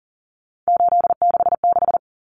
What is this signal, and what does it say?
Morse code, heard as a keyed tone, about 40 wpm, 700 Hz. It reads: 866
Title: Common Cut Numbers - 3-Digit 40wpm